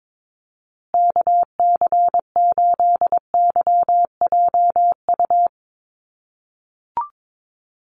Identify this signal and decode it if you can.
Morse code, heard as a keyed tone, about 22 wpm, 700 Hz. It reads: KC8YJU
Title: USA Call Signs Repeated 3x 22wpm